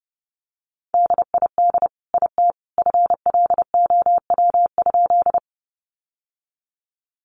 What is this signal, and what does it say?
Morse code, heard as a keyed tone, about 30 wpm, 700 Hz. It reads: DID IT FLOW?
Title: Sentences from Top 1200 Words 30wpm